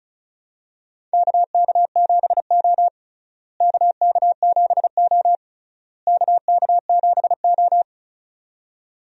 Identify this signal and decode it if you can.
Morse code, heard as a keyed tone, about 36 wpm, 700 Hz. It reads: KK7O KK7O KK7O
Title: USA Call Signs - 4 Characters - Repeated 3x - 35wpm